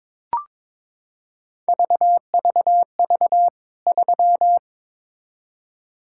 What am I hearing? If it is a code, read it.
Morse code, heard as a keyed tone, about 22 wpm, 700 Hz. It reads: VVV 3
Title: K1USN SST QSO Practice 22wpm